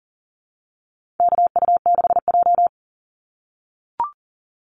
Morse code, heard as a keyed tone, about 40 wpm, 700 Hz. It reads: KU6J
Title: NA SOTA Activator Callsigns 40wpm